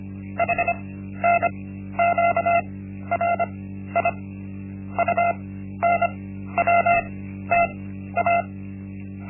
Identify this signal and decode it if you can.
Morse code, heard as a keyed tone, about 25 wpm, spaced as for 13 wpm, 690 Hz. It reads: HNQRI UNWTA
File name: Linksmakalnis-1